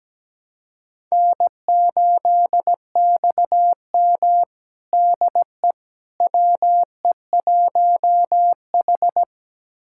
Morse code, heard as a keyed tone, about 17 wpm, 700 Hz. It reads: N8XM DE WE1H
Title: USA Callsign DE Callsign - 4 Characters - 17wpm